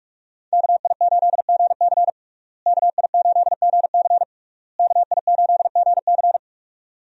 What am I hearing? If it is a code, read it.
Morse code, heard as a keyed tone, about 46 wpm, 700 Hz. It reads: KI8GC KI8GC KI8GC
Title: USA Call Signs - 5 Characters - Repeated 3x - 45wpm